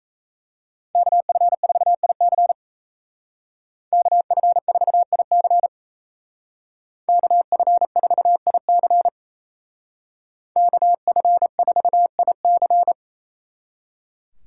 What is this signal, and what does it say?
Morse code, heard as a keyed tone, about 33 wpm, 700 Hz. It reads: KF4IC KF4IC KF4IC KF4IC
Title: Speed Racing - USA 2x2 Call Signs 28wpm